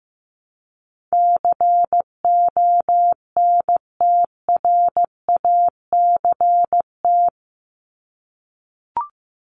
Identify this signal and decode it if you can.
Morse code, heard as a keyed tone, about 15 wpm, 700 Hz. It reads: CONTRACT
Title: Top 1251-1500 Words 15wpm